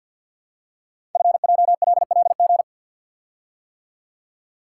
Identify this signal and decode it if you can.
Morse code, heard as a keyed tone, about 48 wpm, 690 Hz. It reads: FJLLG
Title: Code Groups - 5 Letters - 50wpm